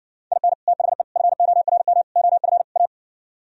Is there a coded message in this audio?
Yes: Morse code from 0.3 to 2.9 s, heard as a keyed tone, about 70 wpm, 700 Hz. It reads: IN THE FORM OF A